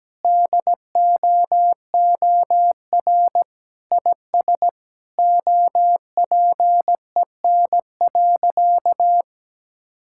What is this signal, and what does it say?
Morse code, heard as a keyed tone, about 17 wpm, 700 Hz. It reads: DOOR IS OPEN.
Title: Sentences from Top 1000 Words - Review Part 6 - 17wpm